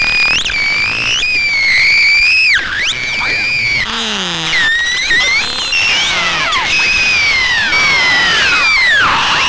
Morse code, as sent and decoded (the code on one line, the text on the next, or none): none